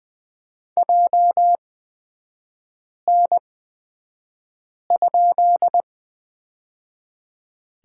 Morse code, heard as a keyed tone, about 20 wpm, spaced as for 5 wpm, 700 Hz. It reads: JN?
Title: 250 Lesson 40e - BK - Three Characters 20wpm